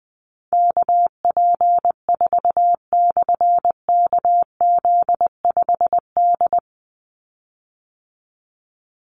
Morse code, heard as a keyed tone, about 20 wpm, 700 Hz. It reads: KP4/KZ5D
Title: Call Signs Repeated 3x - 20wpm